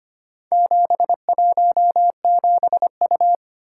Morse code, heard as a keyed tone, about 25 wpm, 700 Hz. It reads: 717U
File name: practice / 25wpm / Four Letter-Number Combinations - 25wpm